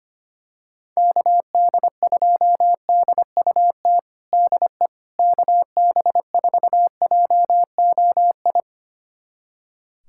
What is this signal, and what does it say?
Morse code, heard as a keyed tone, about 25 wpm, 700 Hz. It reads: KD2DUT DE KB4JOI
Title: USA Callsign DE Callsign - 6 Characters - 25wpm